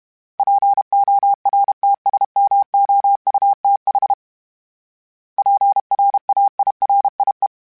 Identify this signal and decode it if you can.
Morse code, heard as a keyed tone, about 32 wpm, 800 Hz. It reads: PORTSMOUTH PRAIRIE
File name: fromSMs / FromSMs mp3s / Cities@30